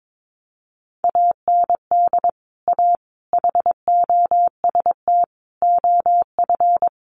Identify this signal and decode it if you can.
Morse code, heard as a keyed tone, about 22 wpm, 700 Hz. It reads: AND A HOST OF
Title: Sets of 5 Words Encore2 22wpm